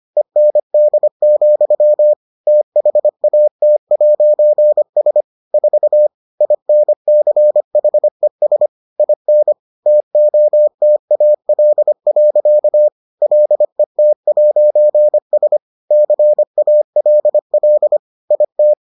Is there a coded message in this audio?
Yes: Morse code from 0.2 to 18.7 s, heard as a keyed tone, about 25 wpm, 600 Hz. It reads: END, THAT'S 4 INCHES IN TOTAL. LET'S CALL IT